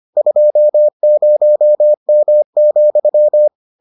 Morse code, heard as a keyed tone, about 25 wpm, 600 Hz. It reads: 20M,